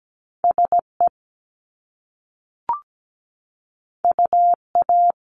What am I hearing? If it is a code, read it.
Morse code, heard as a keyed tone, about 17 wpm, 700 Hz. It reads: SE UA
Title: Top Two Letter Combinations in Words 17wpm